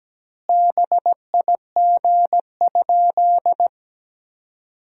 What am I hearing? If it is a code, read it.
Morse code, heard as a keyed tone, about 17 wpm, 700 Hz. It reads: BIG?
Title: 2x Word-Spacing - Sentences from Top 200 Words - 17wpm